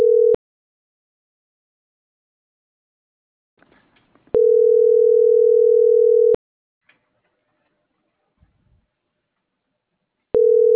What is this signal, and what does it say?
Ringback tone (North America), 440 + 480 Hz.